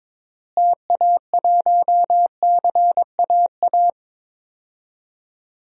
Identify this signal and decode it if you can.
Morse code, heard as a keyed tone, about 22 wpm, 700 Hz. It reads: TA1CAA
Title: DXCC-100 Call Signs - 6 Characters - Repeated 2x - 22wpm